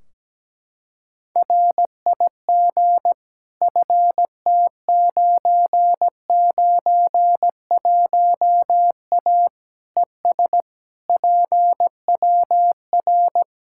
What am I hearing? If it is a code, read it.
Morse code, heard as a keyed tone, about 17 wpm, 700 Hz. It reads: RIG FT991A ES PWR
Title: Practice QSO - Exchange 2 - Info - 17wpm